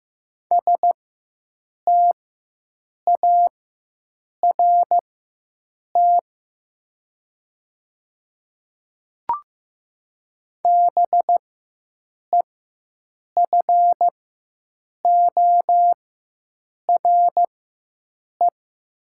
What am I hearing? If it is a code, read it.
Morse code, heard as a keyed tone, about 15 wpm, spaced as for 7 wpm, 700 Hz. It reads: START BEFORE
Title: 4x Character Spacing - Top 300 Words - 15wpm